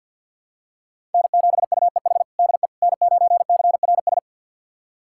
Morse code, heard as a keyed tone, about 50 wpm, 700 Hz. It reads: N7UEH DE N9CRS
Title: USA Callsign DE Callsign - 5 Characters - 50wpm